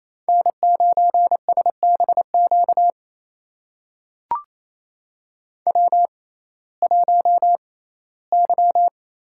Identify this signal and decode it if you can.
Morse code, heard as a keyed tone, about 27 wpm, spaced as for 17 wpm, 700 Hz. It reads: N9SBQ W1Y
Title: USA Call Signs - 6x Character Spacing - 28wpm